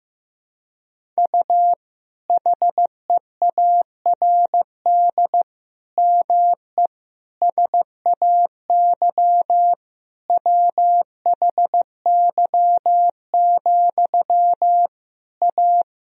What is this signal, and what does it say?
Morse code, heard as a keyed tone, about 16 wpm, 700 Hz. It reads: U HEARD ME SAY WHY, A
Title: Sentences from Top 500 Words and Abbreviations 15wpm